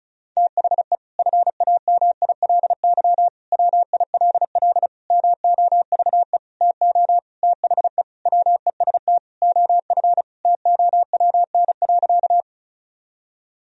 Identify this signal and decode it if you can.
Morse code, heard as a keyed tone, about 36 wpm, 700 Hz. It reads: THE FAMILY WILL MOVE TO THE WEST OF TOWN.
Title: Sentences from Top 1000 Words - Review Part 2 - 35wpm